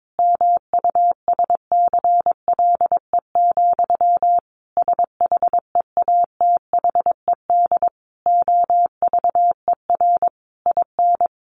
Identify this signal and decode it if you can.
Morse code, heard as a keyed tone, about 22 wpm, 700 Hz. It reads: MUSCLE, SHEATHED OVER IN
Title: A Sound Of Thunder 22wpm